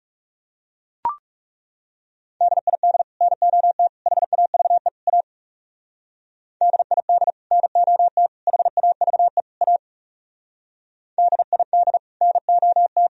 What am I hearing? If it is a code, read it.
Morse code, heard as a keyed tone, about 40 wpm, 700 Hz. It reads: DID NOT HAVE A DID NOT HAVE A DID NOT